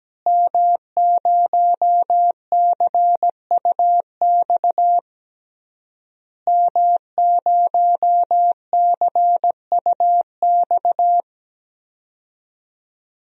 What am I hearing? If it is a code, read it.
Morse code, heard as a keyed tone, about 17 wpm, 700 Hz. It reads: M0CUX M0CUX
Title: DXCC-100 Call Signs - 5 Characters - Repeated 3x - 17wpm